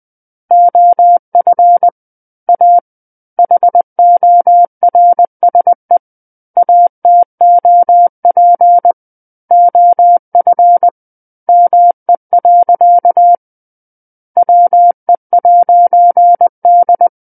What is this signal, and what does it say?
Morse code, heard as a keyed tone, about 20 wpm, 700 Hz. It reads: OF A HORSE ATOP OF ME. WE'D
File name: 20 WPM